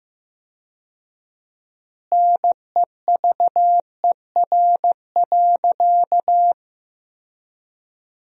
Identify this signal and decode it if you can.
Morse code, heard as a keyed tone, about 15 wpm, 700 Hz. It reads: NEVER.
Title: 4x Word-Spacing - Sentences from Top 400 Words - 15wpm